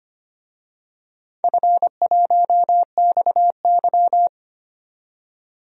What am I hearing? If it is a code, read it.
Morse code, heard as a keyed tone, about 25 wpm, 700 Hz. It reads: F1XY
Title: DXCC-100 Call Signs - 4 Characters - 25wpm